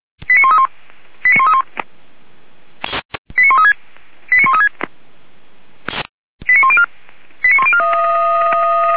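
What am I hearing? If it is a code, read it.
Five-tone selective call (ZVEI), heard as quick runs of five tones, tones of about 70 ms. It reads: 89131 89131 89136 89136 89104 89104
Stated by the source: ZVEI five-tone selective call 89200, received on an SDR, from the feuerwehr-orlamuende.org fire brigade audio archive.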